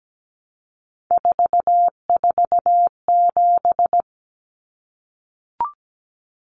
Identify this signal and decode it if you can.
Morse code, heard as a keyed tone, about 17 wpm, 700 Hz. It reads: 447
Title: Common Cut Numbers - 3-Digit 17wpm